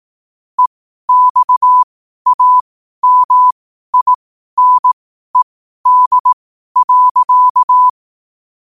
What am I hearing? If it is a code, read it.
Morse code, heard as a keyed tone, about 18 wpm, spaced as for 12 wpm, 1000 Hz. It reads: EXAMINED.
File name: qotdfiles / 12wpm